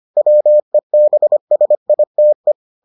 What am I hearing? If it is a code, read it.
Morse code, heard as a keyed tone, about 25 wpm, 600 Hz. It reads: WEBSITE